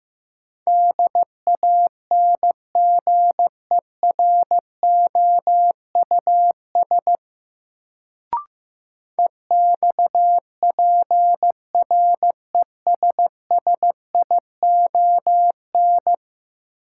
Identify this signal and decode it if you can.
Morse code, heard as a keyed tone, about 15 wpm, 700 Hz. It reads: DANGEROUS EXPRESSION
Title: Sentences from Top 1200 Words - New Word Drill - 15wpm